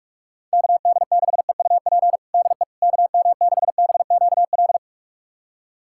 Morse code, heard as a keyed tone, about 42 wpm, 700 Hz. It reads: KD6EUP DE KM6BQL
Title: USA Callsign DE Callsign - 6 Characters - 45wpm